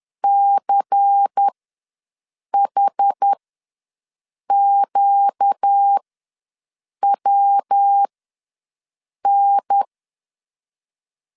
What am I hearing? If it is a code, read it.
Morse code, heard as a keyed tone, about 10 wpm, spaced as for 6 wpm, 800 Hz. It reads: CHQWN
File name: LEZIONI / 5 LBVC / 5 LBVC ws 10wpm 3733